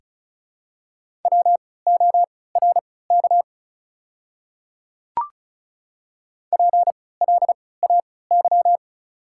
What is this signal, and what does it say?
Morse code, heard as a keyed tone, about 34 wpm, spaced as for 20 wpm, 700 Hz. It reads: WORK PLAY
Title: 3x Character Spacing - Top 300 Words - 35wpm